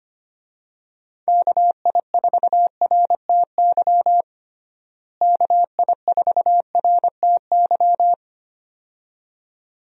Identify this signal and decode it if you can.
Morse code, heard as a keyed tone, about 25 wpm, 700 Hz. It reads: KI4RTY KI4RTY
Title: USA Call Signs - 6 Characters - Repeated 2x - 25wpm